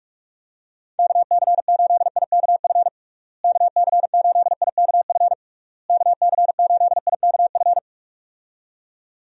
Morse code, heard as a keyed tone, about 46 wpm, 700 Hz. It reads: KC8IKF KC8IKF KC8IKF
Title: USA Call Signs - 6 Characters - Repeated 3x - 45wpm